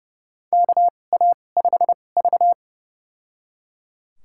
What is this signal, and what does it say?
Morse code, heard as a keyed tone, about 30 wpm, spaced as for 22 wpm, 700 Hz. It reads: KA5V
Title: USA Call Signs - 4 Characters - 2x Character Spacing - 30wpm